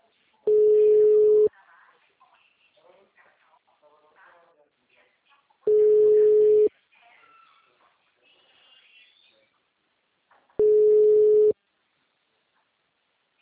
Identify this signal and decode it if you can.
Ringback tone (Europe, 425 Hz), with speech.
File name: out- -2046- - - .77